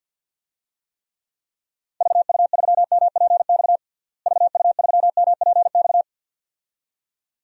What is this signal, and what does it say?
Morse code, heard as a keyed tone, about 47 wpm, 700 Hz. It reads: VU3MPX VU3MPX
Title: DXCC-100 Call Signs - 6 Characters - Repeated 2x - 50wpm